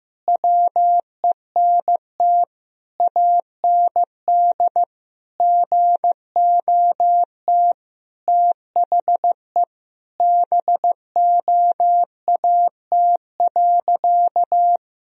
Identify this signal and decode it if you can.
Morse code, heard as a keyed tone, about 15 wpm, 700 Hz. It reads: WENT AND GOT THE BOAT.
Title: Sentences from Top 1000 Words - Review Part 3 - 15wpm